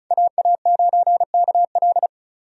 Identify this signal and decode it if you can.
Morse code, heard as a keyed tone, about 36 wpm, 700 Hz. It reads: AA9KL
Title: USA Call Signs - 5 Characters - 35wpm